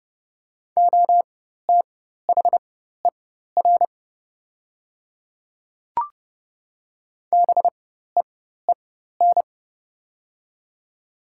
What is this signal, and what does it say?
Morse code, heard as a keyed tone, about 30 wpm, spaced as for 14 wpm, 700 Hz. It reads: OTHER BEEN
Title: 4x Character Spacing - Top 100 Words - 30wpm